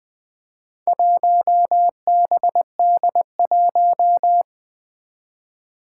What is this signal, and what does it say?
Morse code, heard as a keyed tone, about 20 wpm, 700 Hz. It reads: 1BD1
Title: Four Challenging Letter-Number Combinations 20wpm